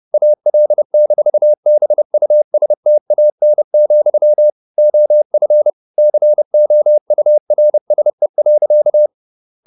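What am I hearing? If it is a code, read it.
Morse code, heard as a keyed tone, about 30 wpm, 600 Hz. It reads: AL-BUSTAN, OF COURSE.
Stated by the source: Morse code at 30 wpm, the novel Never by Ken Follett in English, from CW med Gnister.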